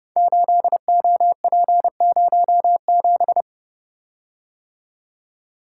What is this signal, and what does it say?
Morse code, heard as a keyed tone, about 30 wpm, 700 Hz. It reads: 8OP07
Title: Code Groups - 5 Letter:Numbers - 30wpm